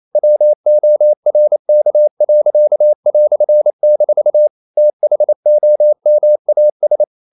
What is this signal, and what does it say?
Morse code, heard as a keyed tone, about 28 wpm, 600 Hz. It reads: WORK."- THOMAS
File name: CW / Practice / EnglishQuotations0228